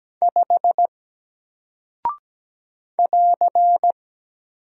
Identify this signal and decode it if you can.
Morse code, heard as a keyed tone, about 17 wpm, 700 Hz. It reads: 5 <AR>
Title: Single Character (Letter:Number:Punctuation:Prosign) Alt 17wpm